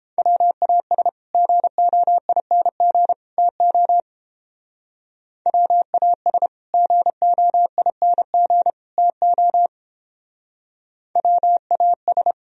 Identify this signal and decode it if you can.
Morse code, heard as a keyed tone, about 29 wpm, 700 Hz. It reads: WAS GOING TO WAS GOING TO WAS